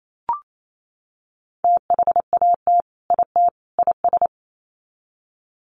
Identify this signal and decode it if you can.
Morse code, heard as a keyed tone, about 28 wpm, 700 Hz. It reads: THAT IT IS
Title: Sets of 3 Words 28wpm